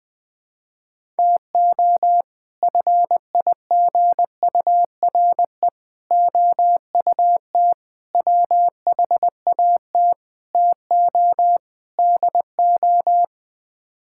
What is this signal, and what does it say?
Morse code, heard as a keyed tone, about 21 wpm, 700 Hz. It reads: TO FIGURE OUT WHAT TO DO
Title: Sets of 6 Words - 20wpm